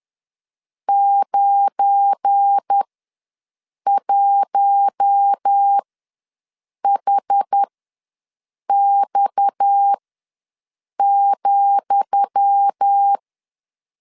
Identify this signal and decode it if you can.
Morse code, heard as a keyed tone, about 10 wpm, spaced as for 6 wpm, 800 Hz. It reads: 91HX,
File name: LEZIONI / 10 SIGNS / 10D COMA ws 10wpm 4314